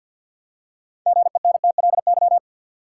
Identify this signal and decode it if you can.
Morse code, heard as a keyed tone, about 52 wpm, 700 Hz. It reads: GENTLY
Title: Six Letter Words - 50wpm